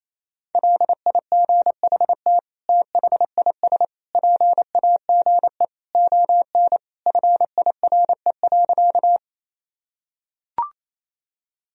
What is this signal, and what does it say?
Morse code, heard as a keyed tone, about 28 wpm, 700 Hz. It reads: LIGHT THIS PAGE ON FIRE.
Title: Sentences from Top 1000 Words - Review Part 1 - 28wpm